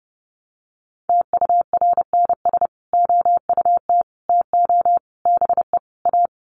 Morse code, heard as a keyed tone, about 30 wpm, 700 Hz. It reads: TURNS OUT TO BE A
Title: Sets of 5 Words Encore 30wpm